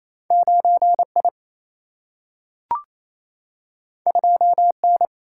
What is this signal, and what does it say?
Morse code, heard as a keyed tone, about 28 wpm, 700 Hz. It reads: 9I 2N
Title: Single-Letter Single-Number 28wpm